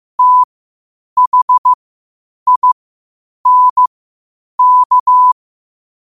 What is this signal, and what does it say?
Morse code, heard as a keyed tone, about 15 wpm, spaced as for 8 wpm, 1000 Hz. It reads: THINK
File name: qotdfiles / 8wpm